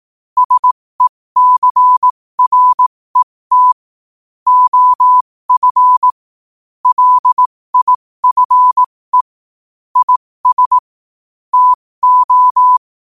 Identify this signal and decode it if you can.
Morse code, heard as a keyed tone, about 18 wpm, spaced as for 15 wpm, 1000 Hz. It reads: SECRET OF LIFE IS TO
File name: qotdfiles / 15wpm